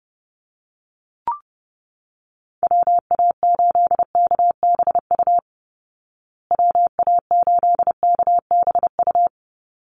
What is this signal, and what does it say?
Morse code, heard as a keyed tone, about 30 wpm, 700 Hz. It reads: WA8KBU WA8KBU
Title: Call Signs Repeated 3x - 30wpm